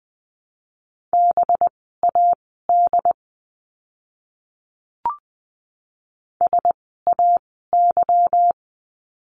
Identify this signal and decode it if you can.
Morse code, heard as a keyed tone, about 20 wpm, spaced as for 15 wpm, 700 Hz. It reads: BAD SAY
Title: 185 Lesson 30f - Letter b - Words 20wpm